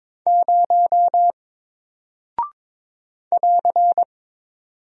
Morse code, heard as a keyed tone, about 22 wpm, 700 Hz. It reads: 0 <AR>
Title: Single Character (Letter:Number:Punctuation:Prosign) 22wpm